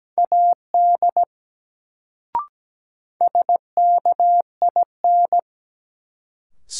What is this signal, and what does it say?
Morse code, heard as a keyed tone, about 17 wpm, 700 Hz. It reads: AD SKIN
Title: Sentences from Top 1300 Words - New Word Drill - 17wpm